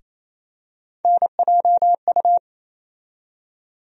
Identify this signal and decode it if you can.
Morse code, heard as a keyed tone, about 27 wpm, 700 Hz. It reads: NJU
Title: USA Call Sign Postfixes - 3 Letters 28wpm